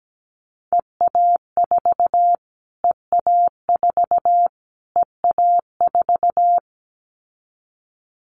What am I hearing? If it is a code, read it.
Morse code, heard as a keyed tone, about 17 wpm, 700 Hz. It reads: EA4 EA4 EA4
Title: Callsign Prefixes for Top 25 DXCC - Repeated 3x - 17wpm